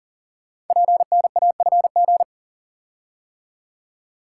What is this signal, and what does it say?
Morse code, heard as a keyed tone, about 37 wpm, 700 Hz. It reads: PNAFG
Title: Code Groups - 5 Letters - 40wpm